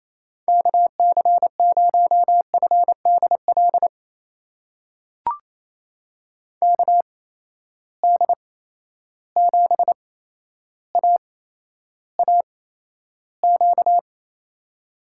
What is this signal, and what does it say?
Morse code, heard as a keyed tone, about 28 wpm, spaced as for 12 wpm, 700 Hz. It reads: KC0FDL KD7AAQ
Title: USA Call Signs - 6 Characters - 8x Character Spacing - 28wpm